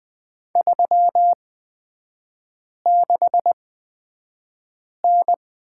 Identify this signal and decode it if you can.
Morse code, heard as a keyed tone, about 20 wpm, spaced as for 5 wpm, 700 Hz. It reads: 36N
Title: 243 Lesson 39e - Letter z - Three Characters 20wpm